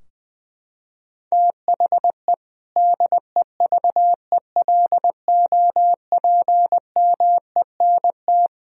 Morse code, heard as a keyed tone, about 20 wpm, 700 Hz. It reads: THE DEVELOPMENT
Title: Sentences from Top 1000 Words - Review Part 4 - 20wpm